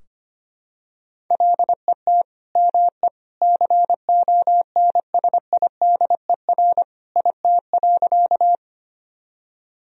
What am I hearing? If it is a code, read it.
Morse code, heard as a keyed tone, about 25 wpm, 700 Hz. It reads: LET ME CONSIDER IT.
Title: Sentences from Top 1000 Words - Review Part 4 - 25wpm